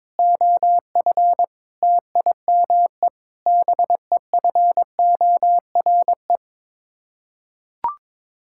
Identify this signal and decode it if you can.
Morse code, heard as a keyed tone, about 22 wpm, 700 Hz. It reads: OF TIME BEFORE
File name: practice / 22wpm / Sets of 8 Words - 22wpm